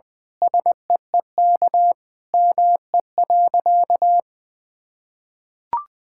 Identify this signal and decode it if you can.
Morse code, heard as a keyed tone, about 20 wpm, 700 Hz. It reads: SEEK ME.
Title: Sentences from Top 1600 Words 20wpm